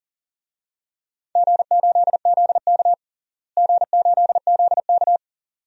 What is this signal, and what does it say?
Morse code, heard as a keyed tone, about 40 wpm, 700 Hz. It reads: G8ZK G8ZK
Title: DXCC-100 Call Signs - 4 Characters - Repeated 2x - 40wpm